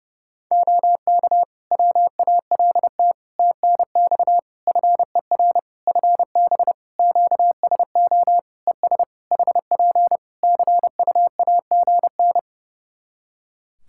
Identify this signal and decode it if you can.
Morse code, heard as a keyed tone, about 30 wpm, 700 Hz. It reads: OK WALT TNX FER FB QSO ES HP CUAGN
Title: Practice QSO - Exchange 4 - Ending - 30wpm